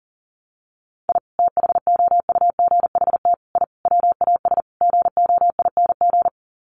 Morse code, heard as a keyed tone, about 40 wpm, 700 Hz. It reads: I THOUGHT I WAS GOING
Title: Sets of 5 Words Encore2 40wpm